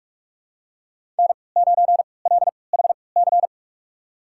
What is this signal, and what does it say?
Morse code, heard as a keyed tone, about 41 wpm, spaced as for 25 wpm, 700 Hz. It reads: N9LHC
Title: USA Call Signs - 3x Character Spacing - 45wpm